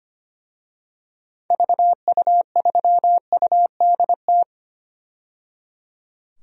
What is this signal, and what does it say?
Morse code, heard as a keyed tone, about 25 wpm, 690 Hz. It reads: VU3UDT
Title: DXCC-100 Call Signs - 6 Characters - 25wpm